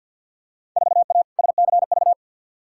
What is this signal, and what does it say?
Morse code, heard as a keyed tone, about 47 wpm, 700 Hz. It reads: 4A SCV
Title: Field Day Exchanges - 50wpm